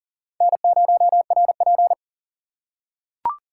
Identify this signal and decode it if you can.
Morse code, heard as a keyed tone, about 40 wpm, 700 Hz. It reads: N0RP
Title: USA Call Signs - 4 Characters - 4x Character Spacing - 40wpm